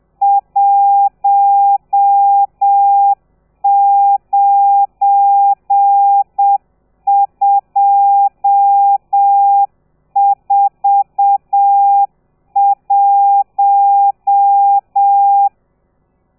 Morse code, heard as a keyed tone, about 7 wpm, 800 Hz. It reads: 19241
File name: telegrafia / i2tfj / Lez12 30